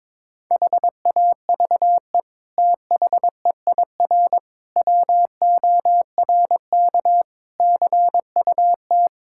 Morse code, heard as a keyed tone, about 21 wpm, 700 Hz. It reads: HAVE THEIR WORK CUT
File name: practice / 22wpm / Sets of 7 Words - 22wpm